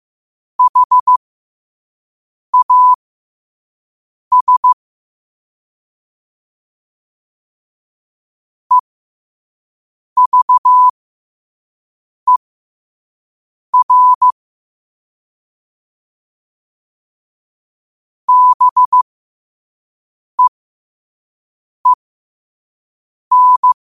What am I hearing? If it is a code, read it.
Morse code, heard as a keyed tone, about 15 wpm, spaced as for 5 wpm, 1000 Hz. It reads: HAS EVER BEEN
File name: qotdfiles / 5wpm